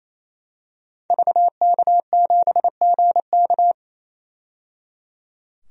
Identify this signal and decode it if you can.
Morse code, heard as a keyed tone, about 28 wpm, 700 Hz. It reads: VK7GK
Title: DXCC-100 Call Signs - 5 Characters - 28wpm